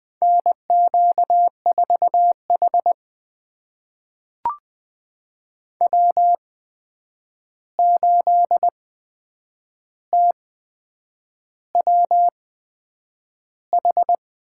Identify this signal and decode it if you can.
Morse code, heard as a keyed tone, about 20 wpm, spaced as for 8 wpm, 700 Hz. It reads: NQ4H W8TWH
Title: USA Call Signs - 8x Character Spacing - 20wpm